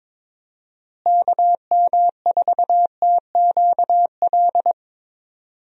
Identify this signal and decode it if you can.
Morse code, heard as a keyed tone, about 22 wpm, 700 Hz. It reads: KM4TQL
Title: USA Call Signs - 6 Characters - Repeated 3x - 22wpm